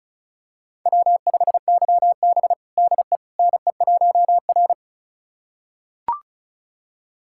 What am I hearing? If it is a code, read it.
Morse code, heard as a keyed tone, about 36 wpm, 700 Hz. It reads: W5YB DE NE1R
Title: USA Callsign DE Callsign - 4 Characters - 35wpm